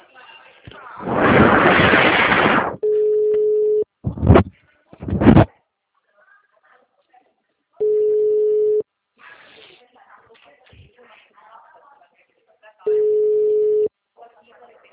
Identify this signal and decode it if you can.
Ringback tone (Europe, 425 Hz), with speech.
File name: out- -2002- - - .60